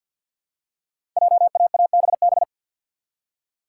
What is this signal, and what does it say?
Morse code, heard as a keyed tone, about 46 wpm, 700 Hz. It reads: JAABB
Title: Code Groups - 5 Letters - 50wpm